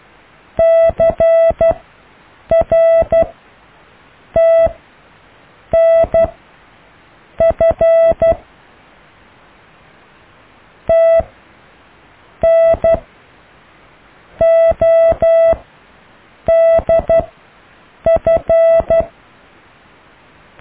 Morse code, heard as a keyed tone, about 12 wpm, spaced as for 6 wpm, 660 Hz. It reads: CRTNF TN ODF
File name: cw / IJLUM+CDRO+EKTPV+BFHNW